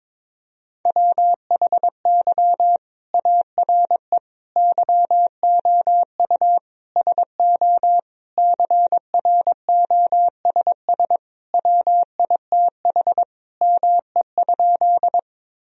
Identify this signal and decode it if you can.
Morse code, heard as a keyed tone, about 22 wpm, 700 Hz. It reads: WHY ARE YOU SO CROSS WITH ME?
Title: Sentences from Top 1000 Words - Review Part 1 - 22wpm